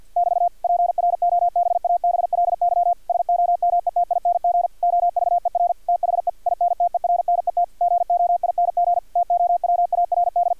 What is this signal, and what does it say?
Morse code, heard as a keyed tone, about 50 wpm, 700 Hz, with radio noise. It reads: <BT> PROBABLY SOMETING OVER THE INTERNET GOING TOWARD
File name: archived / CRSnet